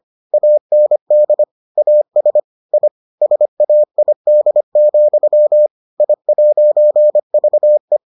Morse code, heard as a keyed tone, about 25 wpm, 600 Hz. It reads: AND AS I SAID, I'VE